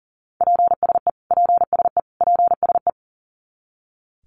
Morse code, heard as a keyed tone, about 37 wpm, 700 Hz. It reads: PSE PSE PSE
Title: Abbreviations Repeated 3x - 40wpm